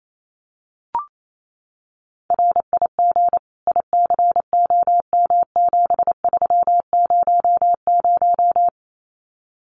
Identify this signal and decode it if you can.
Morse code, heard as a keyed tone, about 28 wpm, 700 Hz. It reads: RIG ICOM7300
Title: Common QSO Phrases Encore 28wpm